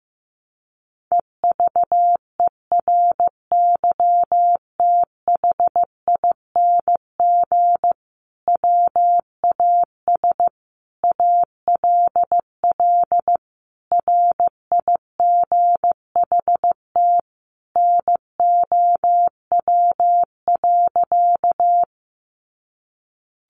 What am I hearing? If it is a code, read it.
Morse code, heard as a keyed tone, about 15 wpm, 700 Hz. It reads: EVERYTHING WAS ALL RIGHT NOW.
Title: Sentences from Top 700 Words 15wpm